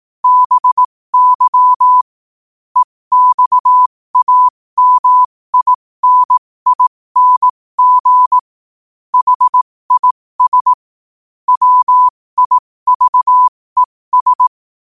Morse code, heard as a keyed tone, about 18 wpm, spaced as for 15 wpm, 1000 Hz. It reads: BY EXAMINING HIS WIVES
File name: qotdfiles / 15wpm